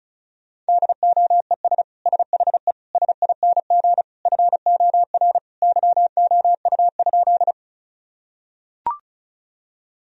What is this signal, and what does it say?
Morse code, heard as a keyed tone, about 36 wpm, 700 Hz. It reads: DOES SHE SING FOR YOU?
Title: Sentences from Top 1000 Words - Review Part 2 - 35wpm